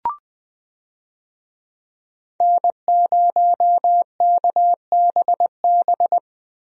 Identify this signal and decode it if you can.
Morse code, heard as a keyed tone, about 20 wpm, 700 Hz. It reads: N0KBB
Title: USA Call Signs - 5 Characters - Repeated 2x - 20wpm